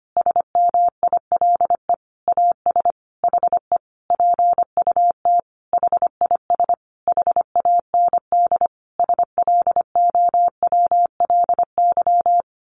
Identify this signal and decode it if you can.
Morse code, heard as a keyed tone, about 25 wpm, 700 Hz. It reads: SMILE AS HE PUT HIS HAND SLOWLY
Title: A Sound Of Thunder 25wpm